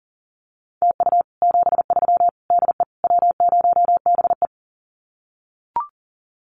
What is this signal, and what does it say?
Morse code, heard as a keyed tone, about 37 wpm, 700 Hz. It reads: TU 73 DE W0BE
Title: Common QSO Phrases Encore 40wpm